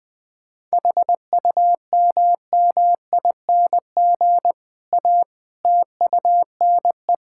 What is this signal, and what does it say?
Morse code, heard as a keyed tone, about 20 wpm, 700 Hz. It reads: HUMMING A TUNE
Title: Difficult Sentences with Hints - 20wpm